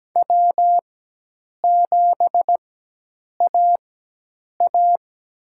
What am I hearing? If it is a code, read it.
Morse code, heard as a keyed tone, about 17 wpm, spaced as for 8 wpm, 700 Hz. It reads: W7AA
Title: USA Call Signs - 4 Characters - 4x Character Spacing - 17wpm